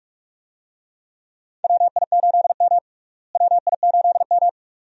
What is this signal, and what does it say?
Morse code, heard as a keyed tone, about 46 wpm, 700 Hz. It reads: WI8M WI8M
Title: USA Call Signs - 4 Characters - Repeated 2x - 45wpm